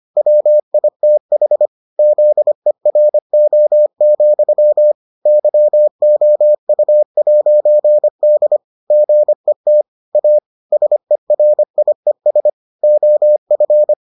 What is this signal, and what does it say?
Morse code, heard as a keyed tone, about 25 wpm, 600 Hz. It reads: WITH ZERO, YOU'D GET A SERIES OF